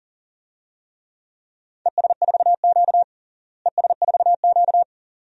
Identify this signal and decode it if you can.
Morse code, heard as a keyed tone, about 40 wpm, 700 Hz. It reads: ES4Q ES4Q
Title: DXCC-100 Call Signs - 4 Characters - Repeated 2x - 40wpm